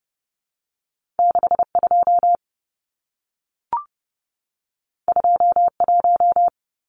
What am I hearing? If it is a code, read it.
Morse code, heard as a keyed tone, about 30 wpm, 700 Hz. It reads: 62 21
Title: Numbers - 2-Digit 30wpm